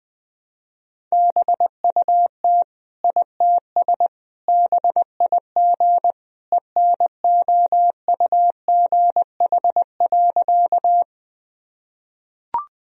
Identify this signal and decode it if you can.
Morse code, heard as a keyed tone, about 21 wpm, 700 Hz. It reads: BUT ITS BIG ENOUGH.
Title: Sentences from Top 1000 Words - Review Part 1 - 20wpm